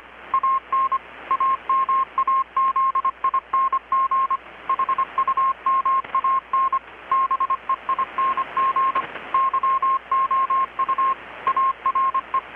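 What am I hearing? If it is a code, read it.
Morse code, heard as a keyed tone, about 25 wpm, 1070 Hz, with radio noise. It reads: AN AMAZING HUMAN BEING YOU ARE
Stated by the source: Morse from KPH's Night of Nights XX maritime radiotelegraphy event, received in Chiapas, Mexico about 3,800 km away on a Kenwood R-600 with a grounded 10 m long-wire antenna.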